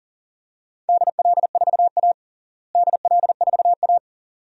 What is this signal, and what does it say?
Morse code, heard as a keyed tone, about 40 wpm, 700 Hz. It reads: DL4A DL4A
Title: DXCC-100 Call Signs - 4 Characters - Repeated 2x - 40wpm